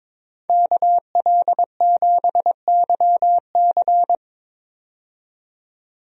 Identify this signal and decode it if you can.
Morse code, heard as a keyed tone, about 22 wpm, 700 Hz. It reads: KL7YC
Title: DXCC-100 Call Signs - 5 Characters - 22wpm